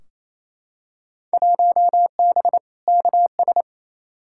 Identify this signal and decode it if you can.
Morse code, heard as a keyed tone, about 28 wpm, 700 Hz. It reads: 1B KS
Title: Field Day Exchanges Encore - 28wpm